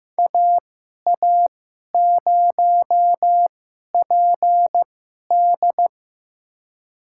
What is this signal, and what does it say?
Morse code, heard as a keyed tone, about 15 wpm, spaced as for 11 wpm, 700 Hz. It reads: AA0PD
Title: USA Call Signs - 5 Characters - 2x Character Spacing - 15wpm